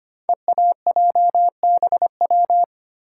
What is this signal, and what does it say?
Morse code, heard as a keyed tone, about 25 wpm, 700 Hz. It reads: EAJBW
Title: Code Groups - 5 Letters - 25wpm